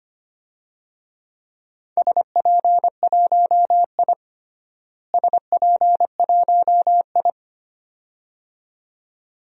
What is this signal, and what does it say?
Morse code, heard as a keyed tone, about 25 wpm, 700 Hz. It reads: SP1I SP1I
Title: DXCC-100 Call Signs - 4 Characters - Repeated 2x - 25wpm